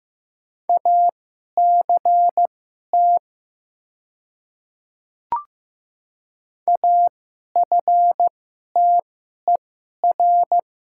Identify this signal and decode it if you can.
Morse code, heard as a keyed tone, about 15 wpm, spaced as for 11 wpm, 700 Hz. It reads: ACT AFTER
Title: 2x Character Spacing - Top 200 Words - 15wpm